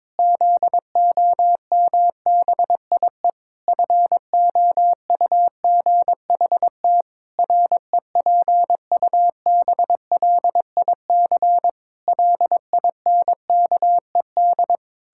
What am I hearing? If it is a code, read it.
Morse code, heard as a keyed tone, about 22 wpm, 700 Hz. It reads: ZOMBIE FOUGHT REPUBLIC LINKED
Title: Pangram Challenge v2 - 22wpm